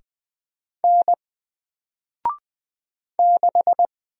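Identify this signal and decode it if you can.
Morse code, heard as a keyed tone, about 20 wpm, 700 Hz. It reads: N 6
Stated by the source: Single letter and number Morse drill at 20 wpm, from N4LCD.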